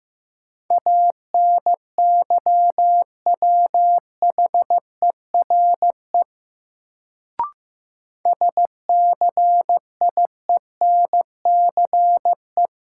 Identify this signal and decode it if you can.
Morse code, heard as a keyed tone, about 15 wpm, 700 Hz. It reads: ANYWHERE SCIENCE
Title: Sentences from Top 1400 Words - New Word Drill Alt - 15wpm